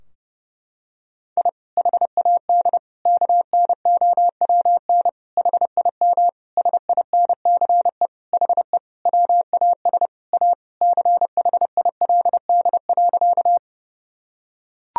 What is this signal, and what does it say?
Morse code, heard as a keyed tone, about 30 wpm, 700 Hz. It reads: I HAD KNOWN HIM SINCE HE WAS A CHILD.
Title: Sentences from Top 1000 Words - Review Part 3 - 30wpm